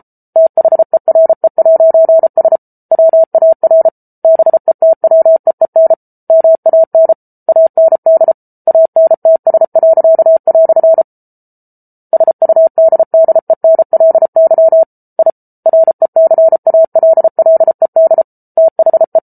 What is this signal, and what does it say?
Morse code, heard as a keyed tone, about 34 wpm, 650 Hz. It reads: THERE'S WAR BETWEEN MAN AND ANTS." SUDDENLY I RECALLED THE